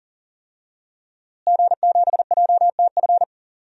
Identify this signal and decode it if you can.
Morse code, heard as a keyed tone, about 40 wpm, 700 Hz. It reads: G7JTF
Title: DXCC-100 Call Signs - 5 Characters - Repeated 3x - 40wpm